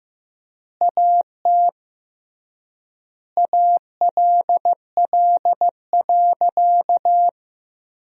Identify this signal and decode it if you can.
Morse code, heard as a keyed tone, about 15 wpm, 700 Hz. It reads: AT ALL.
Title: 3x Word-Spacing - Sentences from Top 500 Words - 15wpm